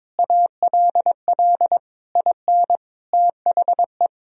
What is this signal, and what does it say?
Morse code, heard as a keyed tone, about 21 wpm, 700 Hz. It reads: ALL IN THE
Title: Sentences from Top 1000 Words - Review Part 2 - 22wpm